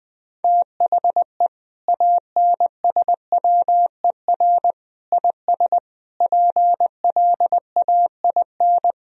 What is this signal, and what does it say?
Morse code, heard as a keyed tone, about 21 wpm, 700 Hz. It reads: THE ANSWER IS PLAIN
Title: Sentences from Top 1000 Words - Review Part 2 - 20wpm